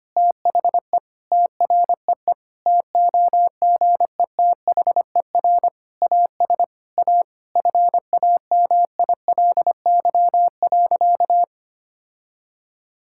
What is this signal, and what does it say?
Morse code, heard as a keyed tone, about 25 wpm, 700 Hz. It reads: THE TREE TOGETHER AS A FAMILY.
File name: practice / 25wpm / Difficult Sentences with Hints - 25wpm